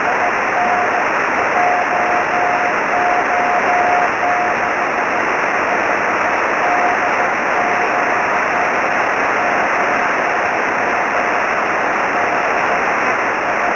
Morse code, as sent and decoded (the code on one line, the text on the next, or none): none